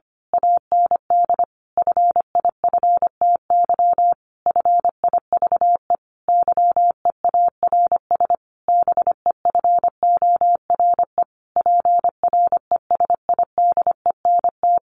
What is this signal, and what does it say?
Morse code, heard as a keyed tone, about 25 wpm, 700 Hz. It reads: AND FIFTY FIVE YEARS BEFORE PRESIDENT
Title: A Sound Of Thunder 25wpm